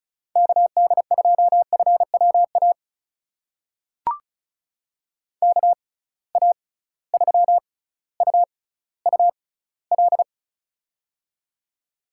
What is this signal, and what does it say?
Morse code, heard as a keyed tone, about 36 wpm, spaced as for 18 wpm, 700 Hz. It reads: KD2FWA KA3UUL
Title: USA Call Signs - 6 Characters - 6x Character Spacing - 35wpm